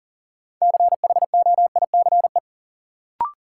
Morse code, heard as a keyed tone, about 38 wpm, 700 Hz. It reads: CHOICE
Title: Sentences from Top 900 Words - New Word Drill - 40wpm